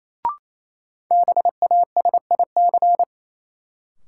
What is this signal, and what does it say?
Morse code, heard as a keyed tone, about 28 wpm, 700 Hz. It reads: BASIC
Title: Sentences from Top 1000 Words - Review Part 3 - New Word Drill - 28wpm